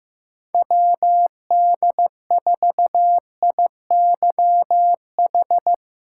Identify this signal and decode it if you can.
Morse code, heard as a keyed tone, about 15 wpm, 700 Hz. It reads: WD4IYH
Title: USA Call Signs - 3x Character Spacing - 15wpm